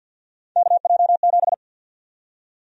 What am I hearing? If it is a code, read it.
Morse code, heard as a keyed tone, about 53 wpm, 700 Hz. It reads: KJ7
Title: USA Call Sign Prefixes - 50wpm